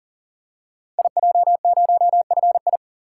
Morse code, heard as a keyed tone, about 38 wpm, 690 Hz. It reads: IJ0FI
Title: Code Groups - 5 Letter:Numbers - 40wpm